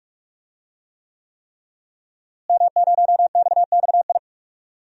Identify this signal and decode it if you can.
Morse code, heard as a keyed tone, about 46 wpm, 700 Hz. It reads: M0XXI
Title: DXCC-100 Call Signs - 5 Characters - Repeated 3x - 45wpm